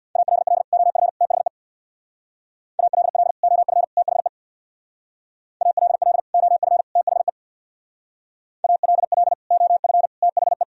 Morse code, heard as a keyed tone, about 63 wpm, 700 Hz. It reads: ALL OF THE ALL OF THE ALL OF THE ALL OF THE